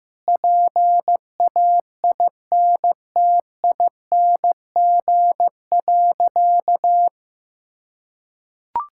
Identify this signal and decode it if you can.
Morse code, heard as a keyed tone, about 15 wpm, 700 Hz. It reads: PAINTING.
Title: Sentences from Top 1000 Words - Review Part 6 - 15wpm